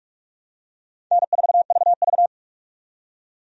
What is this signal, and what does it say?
Morse code, heard as a keyed tone, about 45 wpm, 700 Hz. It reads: N4VV
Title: USA Call Signs - 4 Characters - 45wpm